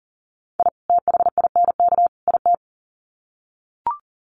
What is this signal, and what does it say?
Morse code, heard as a keyed tone, about 40 wpm, 700 Hz. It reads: I THINK IT
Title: Sets of 3 Words 40wpm